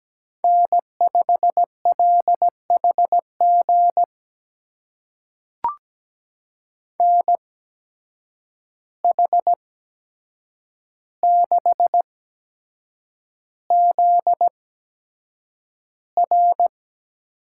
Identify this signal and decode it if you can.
Morse code, heard as a keyed tone, about 17 wpm, spaced as for 7 wpm, 700 Hz. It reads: N5LHG NH6ZR
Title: USA Call Signs - 5 Characters - 8x Character Spacing - 17wpm